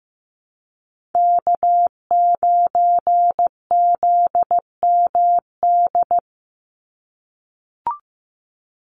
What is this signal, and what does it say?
Morse code, heard as a keyed tone, about 15 wpm, 700 Hz. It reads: K9ZMD
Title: NA SOTA Chaser Callsigns 15wpm